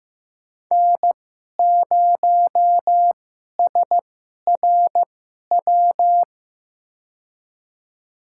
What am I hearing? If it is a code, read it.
Morse code, heard as a keyed tone, about 15 wpm, spaced as for 11 wpm, 700 Hz. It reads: N0SRW
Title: USA Call Signs - 5 Characters - 2x Character Spacing - 15wpm